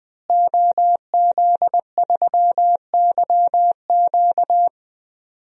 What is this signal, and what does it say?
Morse code, heard as a keyed tone, about 20 wpm, 700 Hz. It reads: OZ3YQ
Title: DXCC-100 Call Signs - 5 Characters - Repeated 2x - 20wpm